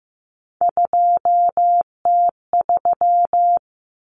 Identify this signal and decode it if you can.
Morse code, heard as a keyed tone, about 15 wpm, 700 Hz. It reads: 2T3
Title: Common Cut Numbers - 3-Digit 15wpm